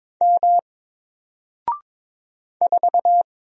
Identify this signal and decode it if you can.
Morse code, heard as a keyed tone, about 22 wpm, 700 Hz. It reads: M 4
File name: Single-Letter-Number-22wpm